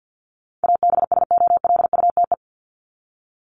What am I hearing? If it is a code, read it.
Morse code, heard as a keyed tone, about 50 wpm, 700 Hz. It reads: ABSOLUTE
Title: LY Suffix - Top 100 Words with Base Words 50wpm